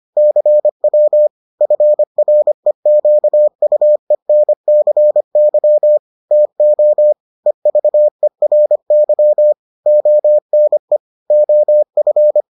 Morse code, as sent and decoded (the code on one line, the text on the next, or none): -.-. .-- / ..-. .-. --.- ..- . -. -.-. -.-- / - --- / . ...- . .-. -.-- / --- -. / --- ..-.
CW FREQUENCY TO EVERY ONE OF